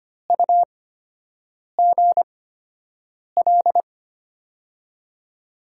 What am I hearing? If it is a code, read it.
Morse code, heard as a keyed tone, about 25 wpm, spaced as for 7 wpm, 700 Hz. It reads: UGL
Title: USA Call Signs - 6 Characters - 8x Character Spacing - 25wpm